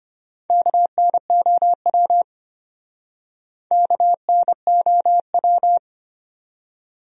Morse code, heard as a keyed tone, about 34 wpm, 700 Hz. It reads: KNOW KNOW